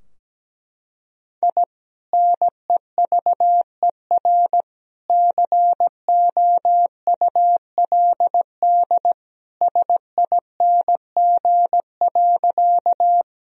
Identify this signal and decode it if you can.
Morse code, heard as a keyed tone, about 17 wpm, 700 Hz. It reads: I NEVER COULD SING.
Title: Sentences from Top 1000 Words - Review Part 2 - 17wpm